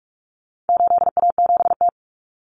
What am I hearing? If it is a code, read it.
Morse code, heard as a keyed tone, about 44 wpm, 700 Hz. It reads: 8A7T
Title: Common Cut Numbers - 4-Digit 45wpm